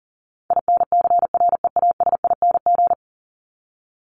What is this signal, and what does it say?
Morse code, heard as a keyed tone, about 37 wpm, 700 Hz. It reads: INCREASING
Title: ING Suffix - Top 100 Words with Base Words 40wpm